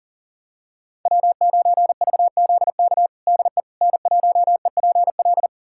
Morse code, heard as a keyed tone, about 37 wpm, 700 Hz. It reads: W9VZK DE N1EPL
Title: USA Callsign DE Callsign - 5 Characters - 40wpm